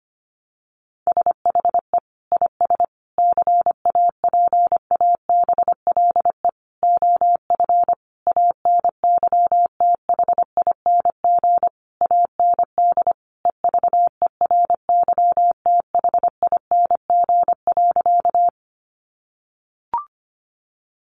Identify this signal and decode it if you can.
Morse code, heard as a keyed tone, about 25 wpm, 700 Hz. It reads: SHE IS CAPABLE OF ANYTHING AND EVERYTHING.
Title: Sentences from Top 1200 Words 25wpm